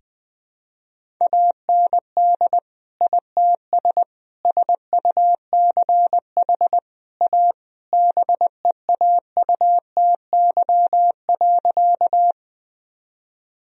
Morse code, heard as a keyed tone, about 20 wpm, 700 Hz. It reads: AND ITS SUCH A BEAUTY.
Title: Sentences from Top 1000 Words - Review Part 2 - 20wpm